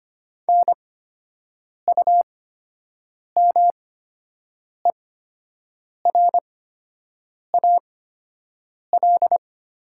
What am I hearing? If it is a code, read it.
Morse code, heard as a keyed tone, about 25 wpm, spaced as for 7 wpm, 700 Hz. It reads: NUMERAL
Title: 8x Character Spacing - Top 500 Words - 25wpm